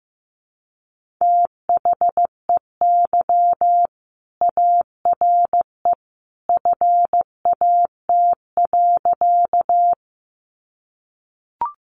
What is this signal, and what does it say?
Morse code, heard as a keyed tone, about 15 wpm, 700 Hz. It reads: THEY ARE FAT.
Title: Sentences from Top 600 Words 15wpm